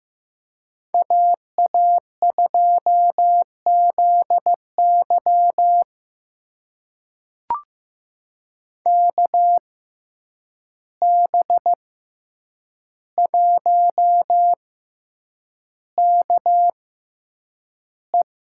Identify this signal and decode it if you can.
Morse code, heard as a keyed tone, about 15 wpm, spaced as for 8 wpm, 700 Hz. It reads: AA2ZY KB1KE
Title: USA Call Signs - 5 Characters - 6x Character Spacing - 15wpm